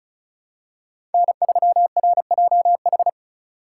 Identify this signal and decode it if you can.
Morse code, heard as a keyed tone, about 34 wpm, 700 Hz. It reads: N3RJH
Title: USA Call Signs - 3x Character Spacing - 35wpm